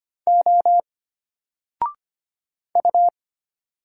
Morse code, heard as a keyed tone, about 25 wpm, 700 Hz. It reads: O U
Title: Single Letter-Number Alt 25wpm